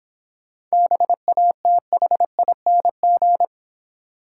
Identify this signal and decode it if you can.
Morse code, heard as a keyed tone, about 26 wpm, 700 Hz. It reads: BATHING